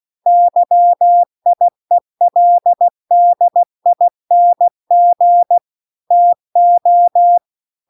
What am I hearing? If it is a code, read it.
Morse code, heard as a keyed tone, about 16 wpm, 700 Hz. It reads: YIELDING TO
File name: CW-The-Big-Twoheaded-River-2-16WPM